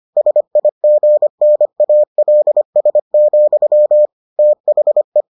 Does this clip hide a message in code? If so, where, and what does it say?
Yes: Morse code from 0.2 to 5.2 s, heard as a keyed tone, about 25 wpm, 600 Hz. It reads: SIGNALS, THE